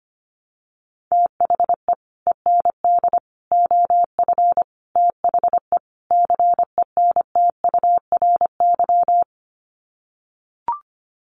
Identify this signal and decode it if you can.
Morse code, heard as a keyed tone, about 25 wpm, 700 Hz. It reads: THE END OF THE CENTURY
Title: Sets of 5 Words Encore2 25wpm